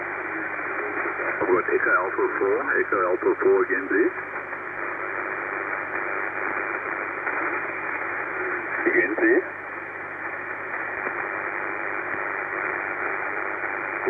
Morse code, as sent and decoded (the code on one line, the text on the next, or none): none